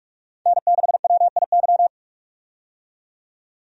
Morse code, heard as a keyed tone, about 43 wpm, 700 Hz. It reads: N6WIY
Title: USA Call Signs - 5 Characters - 45wpm